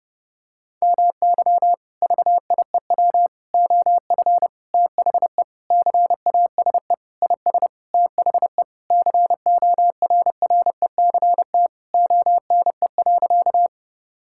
Morse code, heard as a keyed tone, about 30 wpm, 700 Hz. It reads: MY VIEW OF THE CASE IS THE CORRECT ONE.
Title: Sentences from Top 1000 Words - Review Part 6 - 30wpm